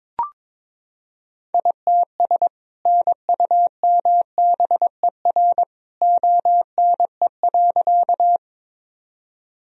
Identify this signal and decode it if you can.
Morse code, heard as a keyed tone, about 22 wpm, 700 Hz. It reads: ITS NUMBER ONE.
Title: Sentences from Top 1000 Words - Review Part 1 - 22wpm